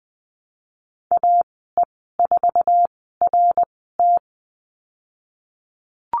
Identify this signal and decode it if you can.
Morse code, heard as a keyed tone, about 20 wpm, 700 Hz. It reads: A E 4 R T
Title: 055 Lesson 10g - Letter r - Call Signs 20wpm